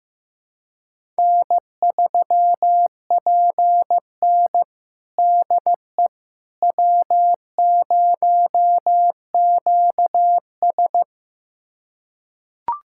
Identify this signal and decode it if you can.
Morse code, heard as a keyed tone, about 15 wpm, 700 Hz. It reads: N3PN DE W0QS
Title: USA Callsign DE Callsign - 4 Characters - 15wpm